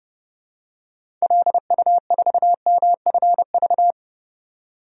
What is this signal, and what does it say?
Morse code, heard as a keyed tone, about 30 wpm, 700 Hz. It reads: LU4MFV
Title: DXCC-100 Call Signs - 6 Characters - 30wpm